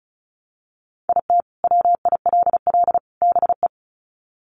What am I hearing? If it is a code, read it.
Morse code, heard as a keyed tone, about 34 wpm, 700 Hz. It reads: IT WILL BE
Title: Sets of 3 Words 35wpm